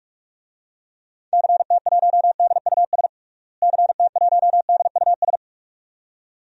Morse code, heard as a keyed tone, about 46 wpm, 700 Hz. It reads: CT1DUS CT1DUS
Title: DXCC-100 Call Signs - 6 Characters - Repeated 2x - 45wpm